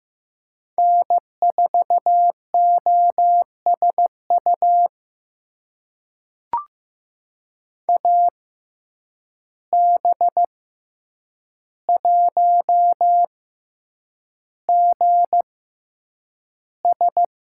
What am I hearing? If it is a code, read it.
Morse code, heard as a keyed tone, about 15 wpm, spaced as for 8 wpm, 700 Hz. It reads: N4OSU AB1GS
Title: USA Call Signs - 5 Characters - 6x Character Spacing - 15wpm